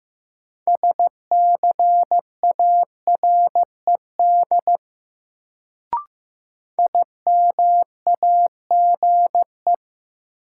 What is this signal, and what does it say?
Morse code, heard as a keyed tone, about 15 wpm, 700 Hz. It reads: SCARED IMAGE
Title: Sentences from Top 700 Words - New Word Drill - 15wpm